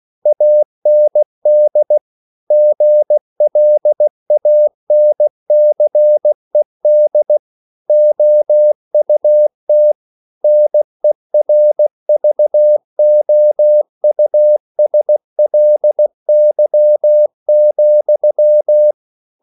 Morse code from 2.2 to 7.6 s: GLANCED

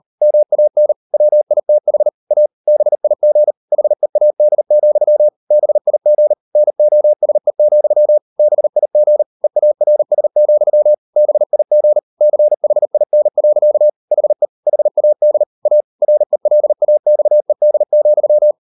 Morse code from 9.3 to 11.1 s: EARS,

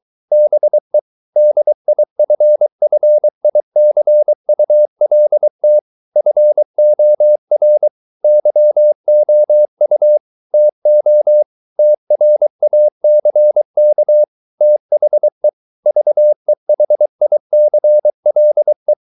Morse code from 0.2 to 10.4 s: BE DIFFICULT FOR YOU

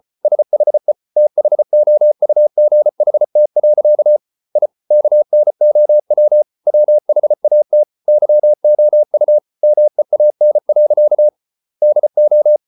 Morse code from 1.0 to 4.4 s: THOUGHT.

